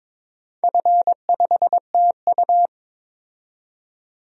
Morse code, heard as a keyed tone, about 22 wpm, 690 Hz. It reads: F5TU
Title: DXCC-100 Call Signs - 4 Characters - 22wpm